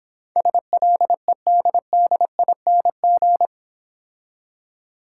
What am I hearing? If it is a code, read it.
Morse code, heard as a keyed tone, about 25 wpm, 700 Hz. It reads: SLEDDING
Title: Easy Words - Speed-Racing with Hints - 20wpm